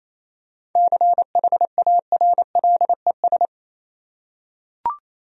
Morse code, heard as a keyed tone, about 28 wpm, 700 Hz. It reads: CHARLES
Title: USA Common Names - 28wpm